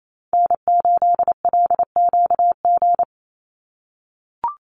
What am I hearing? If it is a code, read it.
Morse code, heard as a keyed tone, about 28 wpm, 700 Hz. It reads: N8LQG
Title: USA Call Signs 28wpm